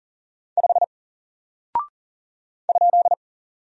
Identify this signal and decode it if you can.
Morse code, heard as a keyed tone, about 40 wpm, 700 Hz. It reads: 5 ?
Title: Single Character (Letter:Number:Punctuation:Prosign) Alt 40wpm